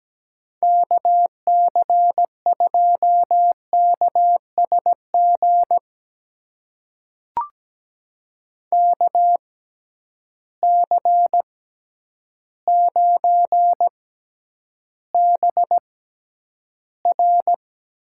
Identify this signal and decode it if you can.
Morse code, heard as a keyed tone, about 17 wpm, spaced as for 9 wpm, 700 Hz. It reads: KC2KSG KC9BR
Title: USA Call Signs - 6 Characters - 6x Character Spacing - 17wpm